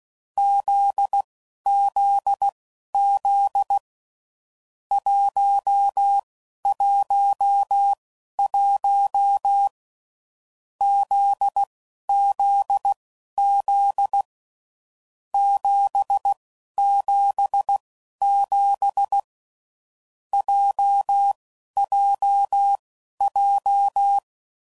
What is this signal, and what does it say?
Morse code, heard as a keyed tone, about 16 wpm, spaced as for 12 wpm, 800 Hz. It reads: ZZZ 111 ZZZ 777 JJJ
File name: Groups / Basic Letter Groups / JZ17